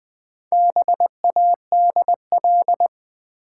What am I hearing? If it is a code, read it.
Morse code, heard as a keyed tone, about 20 wpm, 700 Hz. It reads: BADL
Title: Build A Word - 1 Letter At A Time - 3 to 5-Letter Words - 20wpm